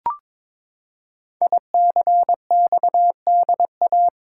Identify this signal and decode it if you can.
Morse code, heard as a keyed tone, about 22 wpm, 690 Hz. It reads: ICXDA
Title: Code Groups - 5 Letter:Numbers - 22wpm